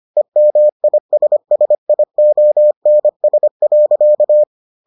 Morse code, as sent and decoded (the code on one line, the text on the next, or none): . -- .. ... ... .. --- -. ... .-.-.-
EMISSIONS.